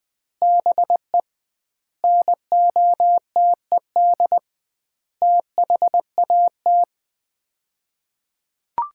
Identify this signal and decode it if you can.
Morse code, heard as a keyed tone, about 20 wpm, 700 Hz. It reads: BE NOTED THAT
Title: 2x Word-Spacing - Sets of 5 Words - 20wpm